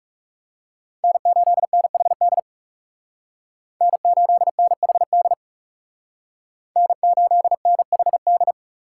Morse code, heard as a keyed tone, about 37 wpm, 700 Hz. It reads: N8NHD N8NHD N8NHD
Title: Speed Racing - USA 1x3 Call Signs 30wpm